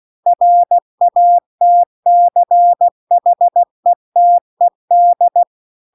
Morse code, heard as a keyed tone, about 16 wpm, 700 Hz. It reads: RATCHETED